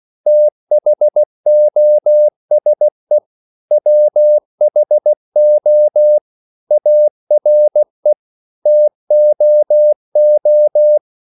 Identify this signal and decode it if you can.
Morse code, heard as a keyed tone, about 16 wpm, 600 Hz. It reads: THOSE WHO ARE TOO
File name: CW / Practice / EnglishQuotations0216